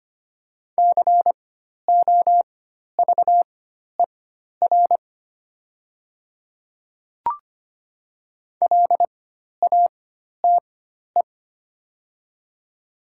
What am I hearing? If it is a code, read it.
Morse code, heard as a keyed tone, about 25 wpm, spaced as for 12 wpm, 700 Hz. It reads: COVER LATE
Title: 4x Character Spacing - Top 300 Words - 25wpm